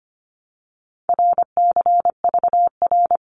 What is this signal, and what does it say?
Morse code, heard as a keyed tone, about 25 wpm, 700 Hz. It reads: RCVR
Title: Abbreviations Repeated 3x - 25wpm